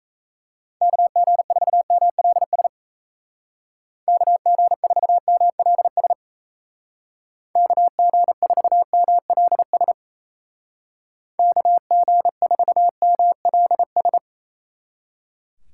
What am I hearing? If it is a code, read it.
Morse code, heard as a keyed tone, about 37 wpm, 700 Hz. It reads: KG4MLS KG4MLS KG4MLS KG4MLS